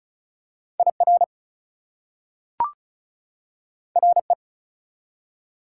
Morse code, heard as a keyed tone, about 36 wpm, 700 Hz. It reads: IR RE
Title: Prefixes and Suffixes 35wpm